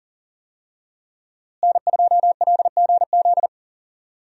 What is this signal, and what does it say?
Morse code, heard as a keyed tone, about 40 wpm, 700 Hz. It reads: N2LGZ
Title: USA Call Signs - 5 Characters - Repeated 3x - 40wpm